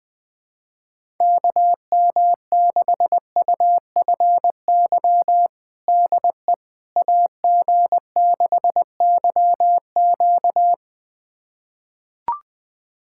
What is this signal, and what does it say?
Morse code, heard as a keyed tone, about 20 wpm, 700 Hz. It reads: KM6UFY DE AG6YQ
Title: USA Callsign DE Callsign 20wpm